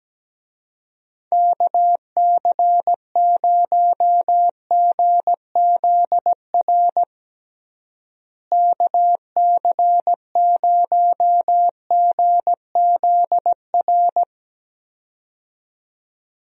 Morse code, heard as a keyed tone, about 17 wpm, 700 Hz. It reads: KC0GZR KC0GZR
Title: USA Call Signs - 6 Characters - Repeated 2x - 17wpm